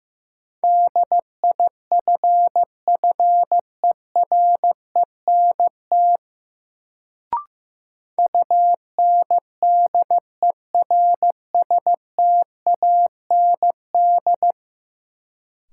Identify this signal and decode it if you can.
Morse code, heard as a keyed tone, about 15 wpm, 700 Hz. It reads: DIFFERENT UNDERSTAND
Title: Sentences from Top 600 Words - New Word Drill - 15wpm